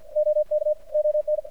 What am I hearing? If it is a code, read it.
Morse code, heard as a keyed tone, about 52 wpm, 600 Hz. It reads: OK ON
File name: archived / CRSnet